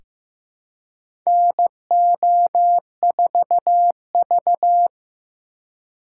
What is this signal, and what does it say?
Morse code, heard as a keyed tone, about 15 wpm, 700 Hz. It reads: NO4V
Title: USA Call Signs - 4x Character Spacing - 15wpm